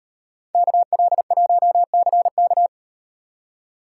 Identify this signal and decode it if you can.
Morse code, heard as a keyed tone, about 35 wpm, 700 Hz. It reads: KL1CK